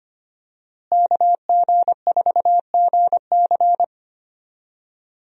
Morse code, heard as a keyed tone, about 25 wpm, 700 Hz. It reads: KG4GC
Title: USA Call Signs - 5 Characters - Repeated 3x - 25wpm